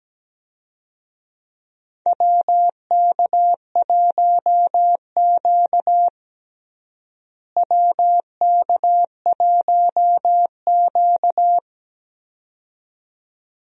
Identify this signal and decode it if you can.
Morse code, heard as a keyed tone, about 17 wpm, 700 Hz. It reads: WK1Q WK1Q
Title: USA Call Signs - 4 Characters - Repeated 2x - 17wpm